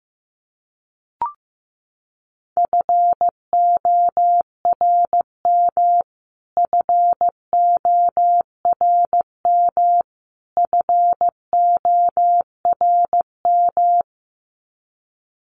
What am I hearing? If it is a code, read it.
Morse code, heard as a keyed tone, about 15 wpm, 700 Hz. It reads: FORM FORM FORM
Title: Top 300 Words Repeated 15wpm